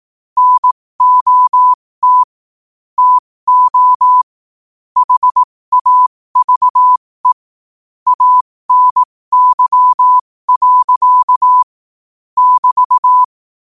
Morse code, heard as a keyed tone, about 18 wpm, spaced as for 15 wpm, 1000 Hz. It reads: NOT TO HAVE ANY. <BT>
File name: 15wpm